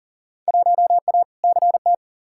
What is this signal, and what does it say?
Morse code, heard as a keyed tone, about 40 wpm, 700 Hz. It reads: 1A CT
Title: Field Day Exchanges - 40wpm